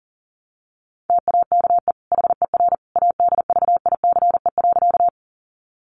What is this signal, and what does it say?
Morse code, heard as a keyed tone, about 37 wpm, 700 Hz. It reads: TAKE HER ADVICE.
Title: Sentences from Top 1100 Words 40wpm